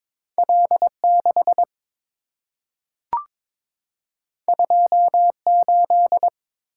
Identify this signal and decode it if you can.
Morse code, heard as a keyed tone, about 22 wpm, 690 Hz. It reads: L6 28
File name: practice / 22wpm / Two Letter-Number Combinations - 22wpm